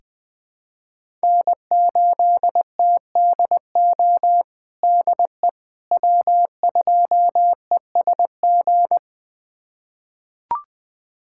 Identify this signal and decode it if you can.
Morse code, heard as a keyed tone, about 20 wpm, 700 Hz. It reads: N8TDO DE W2ESG
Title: USA Callsign DE Callsign - 5 Characters - 20wpm